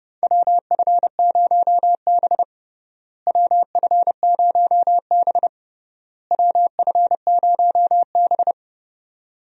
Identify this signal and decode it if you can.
Morse code, heard as a keyed tone, about 30 wpm, 700 Hz. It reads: WF0B WF0B WF0B
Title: USA Call Signs - 4 Characters - Repeated 3x - 30wpm